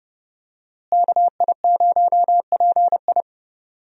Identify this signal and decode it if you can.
Morse code, heard as a keyed tone, about 31 wpm, 700 Hz. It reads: KI0PI
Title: USA Call Signs - 5 Characters - Repeated 2x - 30wpm